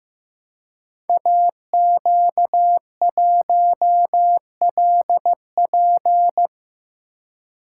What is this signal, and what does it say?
Morse code, heard as a keyed tone, about 15 wpm, 700 Hz. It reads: AQ1LP
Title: Code Groups - 5 Letter:Numbers - 15wpm